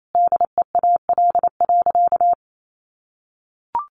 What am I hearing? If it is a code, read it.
Morse code, heard as a keyed tone, about 28 wpm, 700 Hz. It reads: DEAL.
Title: Sentences from Top 1100 Words 28wpm